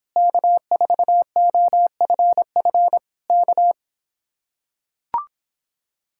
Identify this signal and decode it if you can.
Morse code, heard as a keyed tone, about 26 wpm, 700 Hz. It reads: K4OFF K